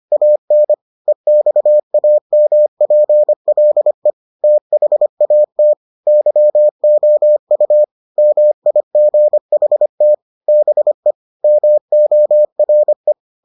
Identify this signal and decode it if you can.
Morse code, heard as a keyed tone, about 25 wpm, 600 Hz. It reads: AN EXAMPLE THAT YOU MIGHT BE MORE